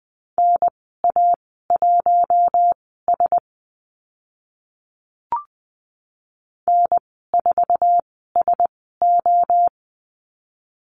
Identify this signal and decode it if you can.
Morse code, heard as a keyed tone, about 20 wpm, spaced as for 14 wpm, 700 Hz. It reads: NA1S N4SO
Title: 069 Lesson 12g - Letter d - Call Signs 20wpm